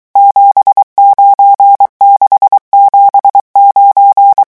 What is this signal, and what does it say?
Morse code, heard as a keyed tone, about 23 wpm, 800 Hz. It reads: 79679
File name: morse mp3 / revision11 chiffres vitesse 22